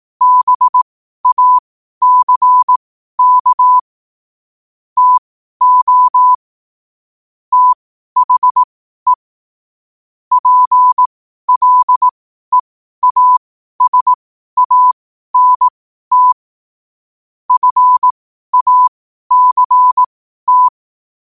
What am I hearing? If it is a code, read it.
Morse code, heard as a keyed tone, about 18 wpm, spaced as for 12 wpm, 1000 Hz. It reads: BACK TO THE PLEASANT FACT
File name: qotdfiles / 12wpm